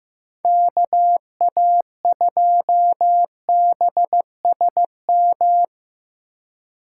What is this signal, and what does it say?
Morse code, heard as a keyed tone, about 15 wpm, 700 Hz. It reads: KA2BSM
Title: USA Call Signs - 6 Characters - 4x Character Spacing - 15wpm